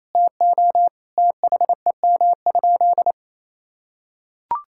Morse code, heard as a keyed tone, about 28 wpm, 700 Hz. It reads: TO THEM?
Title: Sentences from Top 1600 Words 28wpm